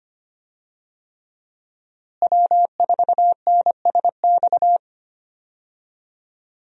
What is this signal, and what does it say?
Morse code, heard as a keyed tone, about 25 wpm, 700 Hz. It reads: W4NSX
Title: USA Call Signs - 5 Characters - Repeated 3x - 25wpm